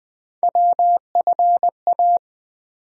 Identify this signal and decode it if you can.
Morse code, heard as a keyed tone, about 20 wpm, 700 Hz. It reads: WFA
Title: USA Call Sign Postfixes - 3 Letters 20wpm